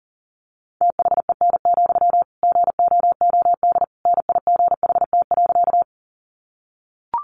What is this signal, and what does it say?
Morse code, heard as a keyed tone, about 40 wpm, 700 Hz. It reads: THEN, GOOD NIGHT.
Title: Sentences from Top 300 Words 40wpm